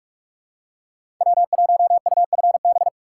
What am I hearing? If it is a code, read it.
Morse code, heard as a keyed tone, about 46 wpm, 700 Hz. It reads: W1UFB
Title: USA Call Signs - 5 Characters - 4x Character Spacing - 45wpm